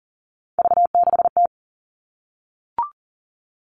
Morse code, heard as a keyed tone, about 40 wpm, 700 Hz. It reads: V6T
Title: All Cut Numbers - 3-Digit 40wpm